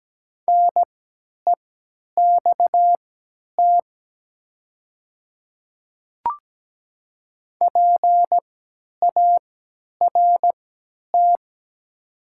Morse code, heard as a keyed tone, about 17 wpm, spaced as for 10 wpm, 700 Hz. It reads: NEXT PART
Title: 3x Character Spacing - Top 300 Words - 17wpm